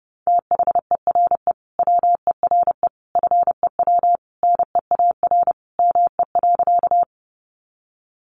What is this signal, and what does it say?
Morse code, heard as a keyed tone, about 30 wpm, 700 Hz. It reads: THERE WERE FEW NEAR ME.
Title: Sentences from Top 300 Words 30wpm